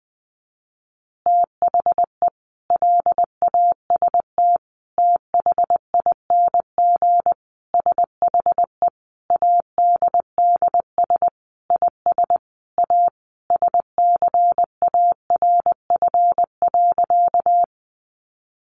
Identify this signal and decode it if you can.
Morse code, heard as a keyed tone, about 20 wpm, 700 Hz. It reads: THE LAST THING SHE ADDS IS A SCARF.
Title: Easy Phrases 20wpm